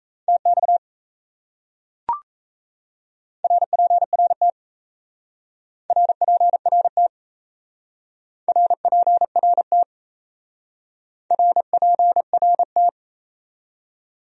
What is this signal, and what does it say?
Morse code, heard as a keyed tone, about 36 wpm, 700 Hz. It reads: TX RPRT RPRT RPRT RPRT